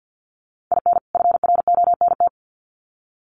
Morse code, heard as a keyed tone, about 50 wpm, 700 Hz. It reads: IN FRONT
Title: Sets of 2 Words 50wpm